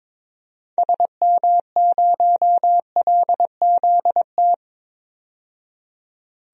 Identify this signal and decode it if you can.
Morse code, heard as a keyed tone, about 22 wpm, 700 Hz. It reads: SM0LZT
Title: DXCC-100 Call Signs - 6 Characters - 22wpm